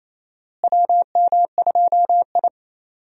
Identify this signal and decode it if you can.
Morse code, heard as a keyed tone, about 28 wpm, 700 Hz. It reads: WM2I
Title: USA Call Signs - 4 Characters - Repeated 2x - 28wpm